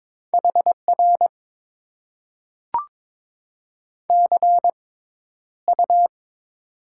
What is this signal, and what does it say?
Morse code, heard as a keyed tone, about 22 wpm, spaced as for 11 wpm, 700 Hz. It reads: HR CU
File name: Top 90 QSO Elements - 6x Character Spacing - 22wpm